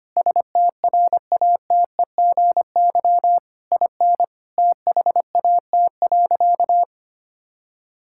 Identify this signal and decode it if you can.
Morse code, heard as a keyed tone, about 25 wpm, 700 Hz. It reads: STRATEGY IN THAT.
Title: Sentences from Top 1000 Words - Review Part 5 - 25wpm